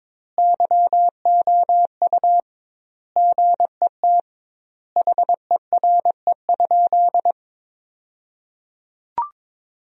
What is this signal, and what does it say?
Morse code, heard as a keyed tone, about 22 wpm, 700 Hz. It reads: YOU GET HERE?
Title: 2x Word-Spacing - Sentences from Top 200 Words - 22wpm